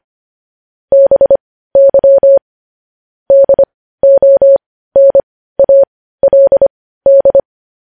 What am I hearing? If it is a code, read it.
Morse code, heard as a keyed tone, about 25 wpm, spaced as for 15 wpm, 550 Hz. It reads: BY DONALD
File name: podcast / News headlines-1x-25x15